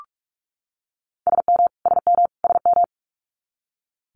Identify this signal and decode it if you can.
Morse code, heard as a keyed tone, about 46 wpm, 700 Hz. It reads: SM SM SM
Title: Multi-letter Phonemes Repeated 3x 45wpm